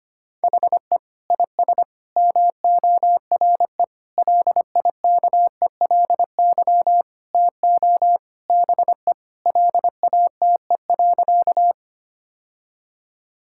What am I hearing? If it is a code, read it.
Morse code, heard as a keyed tone, about 25 wpm, 700 Hz. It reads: HE IS MORE LIKELY TO BE LATE.
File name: practice / 25wpm / Sentences with 4-Word Phrases - 25wpm